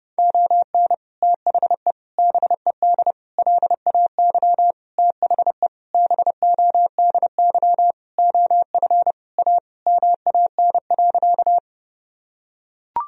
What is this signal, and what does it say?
Morse code, heard as a keyed tone, about 30 wpm, 700 Hz. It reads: ON THE BED LAY THE BODY OF A MAN.
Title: Sentences from Top 1000 Words - Review Part 2 - 30wpm